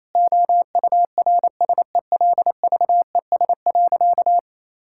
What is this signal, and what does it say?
Morse code, heard as a keyed tone, about 28 wpm, 700 Hz. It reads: OURSELVES.
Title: Sentences from Top 1600 Words 28wpm